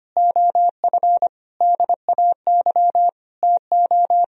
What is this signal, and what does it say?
Morse code, heard as a keyed tone, about 25 wpm, 700 Hz. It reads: OF DAY TO
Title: Sentences from Top 1000 Words - Review Part 1 - 25wpm